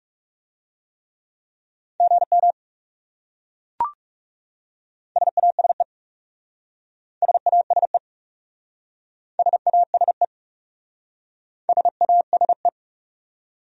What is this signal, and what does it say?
Morse code, heard as a keyed tone, about 40 wpm, 700 Hz. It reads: GM SASE SASE SASE SASE